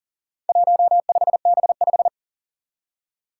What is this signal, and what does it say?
Morse code, heard as a keyed tone, about 39 wpm, 700 Hz. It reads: 15B5
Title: Four Challenging Letter-Number Combinations 40wpm